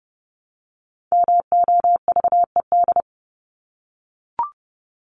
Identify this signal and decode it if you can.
Morse code, heard as a keyed tone, about 30 wpm, 700 Hz. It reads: MOVED
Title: ED Suffix - Top 100 Words with Base Words 30wpm